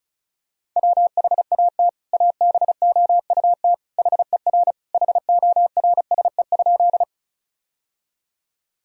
Morse code, heard as a keyed tone, about 34 wpm, 700 Hz. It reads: WHAT ABOUT HER HORSE?
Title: Sentences from Top 1000 Words - Review Part 1 - 35wpm